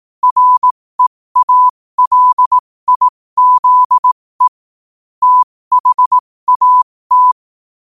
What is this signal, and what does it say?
Morse code, heard as a keyed tone, about 18 wpm, spaced as for 15 wpm, 1000 Hz. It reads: REALIZE THAT
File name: qotdfiles / 15wpm